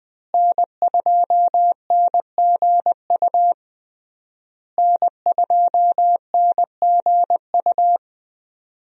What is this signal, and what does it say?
Morse code, heard as a keyed tone, about 20 wpm, 700 Hz. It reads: N2NGU N2NGU
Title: USA Call Signs - 5 Characters - Repeated 3x - 20wpm